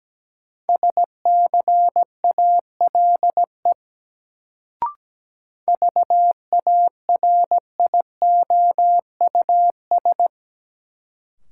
Morse code, heard as a keyed tone, about 17 wpm, 700 Hz. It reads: SCALE VARIOUS
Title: Sentences from Top 1000 Words - Review Part 4 - New Word Drill - 17wpm